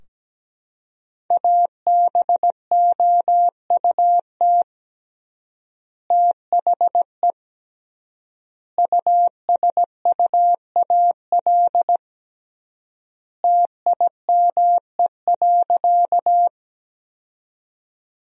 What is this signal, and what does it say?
Morse code, heard as a keyed tone, about 17 wpm, 700 Hz. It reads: ABOUT THE USUAL TIME.
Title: 3x Word-Spacing - Sentences from Top 400 Words - 17wpm